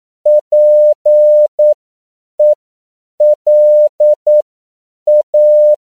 Morse code, heard as a keyed tone, about 9 wpm, spaced as for 7 wpm, 600 Hz. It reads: PELA